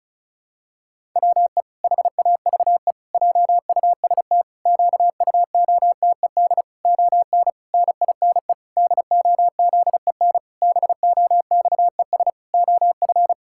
Morse code, heard as a keyed tone, about 36 wpm, 700 Hz. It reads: WE HAVE JUST QUOTED ON NINE DOZEN BOXES OF
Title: Pangram Challenge v2 - 35wpm